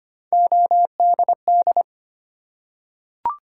Morse code, heard as a keyed tone, about 25 wpm, 700 Hz. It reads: ODD
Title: Three Letter Words - 25wpm